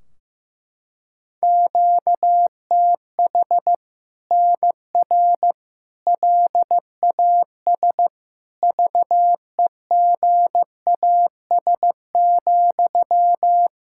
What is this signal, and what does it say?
Morse code, heard as a keyed tone, about 15 wpm, 700 Hz. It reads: QTH NR LAS VEGAS,
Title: Practice QSO - Exchange 1 - Beginning - 15wpm